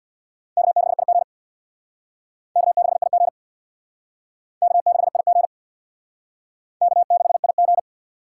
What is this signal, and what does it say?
Morse code, heard as a keyed tone, about 63 wpm, 700 Hz. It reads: K6IG K6IG K6IG K6IG